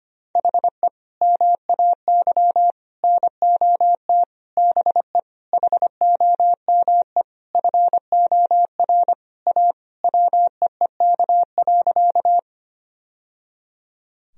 Morse code, heard as a keyed tone, about 25 wpm, 700 Hz. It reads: HE MAY NOT BE HOME FOR A WEEK.
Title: Sentences from Top 1000 Words - Review Part 2 - 25wpm